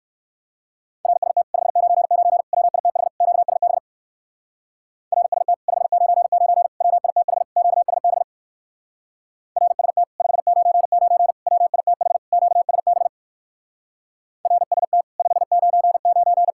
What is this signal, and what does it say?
Morse code, heard as a keyed tone, about 59 wpm, 700 Hz. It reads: RST 599 WITH QSB RST 599 WITH QSB RST 599 WITH QSB RST 599